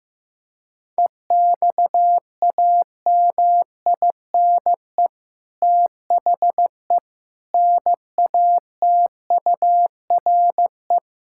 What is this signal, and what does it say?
Morse code, heard as a keyed tone, about 15 wpm, 700 Hz. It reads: EXAMINE THE NATURE
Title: Sentences from Top 1600 Words 15wpm